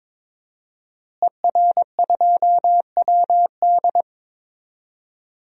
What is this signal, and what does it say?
Morse code, heard as a keyed tone, about 21 wpm, 700 Hz. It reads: ER2WD
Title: DXCC-100 Call Signs - 5 Characters - Repeated 3x - 22wpm